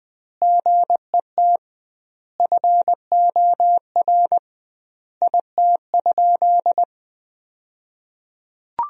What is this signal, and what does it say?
Morse code, heard as a keyed tone, about 20 wpm, 700 Hz. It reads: GET FOR IT?
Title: 2x Word-Spacing - Sentences from Top 400 Words - 20wpm